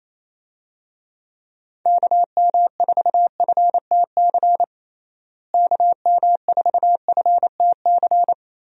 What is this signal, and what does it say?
Morse code, heard as a keyed tone, about 27 wpm, 700 Hz. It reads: KM4FTC KM4FTC
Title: USA Call Signs - 6 Characters - Repeated 3x - 28wpm